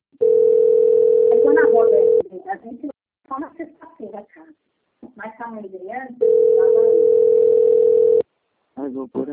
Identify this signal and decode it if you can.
Ringback tone (North America), 440 + 480 Hz, with speech.
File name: out- - - - - .8187